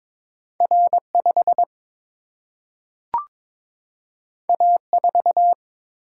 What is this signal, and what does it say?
Morse code, heard as a keyed tone, about 22 wpm, 700 Hz. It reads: R 5 A 4
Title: Single-Letter Single-Number 22wpm